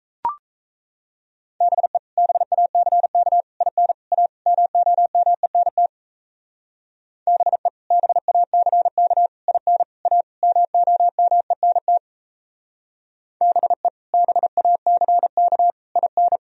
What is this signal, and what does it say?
Morse code, heard as a keyed tone, about 37 wpm, 700 Hz. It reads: BE BACK IN A MOMENT BE BACK IN A MOMENT BE BACK IN